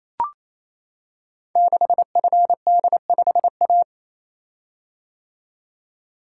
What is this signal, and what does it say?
Morse code, heard as a keyed tone, about 28 wpm, 700 Hz. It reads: 6FD5A
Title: Code Groups - 5 Letter:Numbers - 28wpm